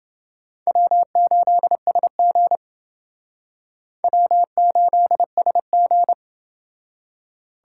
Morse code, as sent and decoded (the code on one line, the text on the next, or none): .-- ---.. ... --. / .-- ---.. ... --.
W8SG W8SG